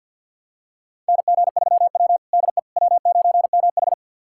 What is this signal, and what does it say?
Morse code, heard as a keyed tone, about 46 wpm, 700 Hz. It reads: NG3W DE W9MH
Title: USA Callsign DE Callsign - 4 Characters - 50wpm